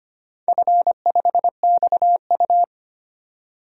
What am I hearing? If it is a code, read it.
Morse code, heard as a keyed tone, about 25 wpm, 700 Hz. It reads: F5XU
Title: DXCC-100 Call Signs - 4 Characters - 25wpm